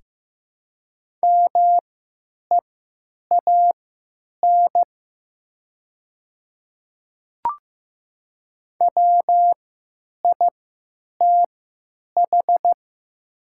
Morse code, heard as a keyed tone, about 15 wpm, spaced as for 9 wpm, 700 Hz. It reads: MEAN WITH
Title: 3x Character Spacing - Top 300 Words - 15wpm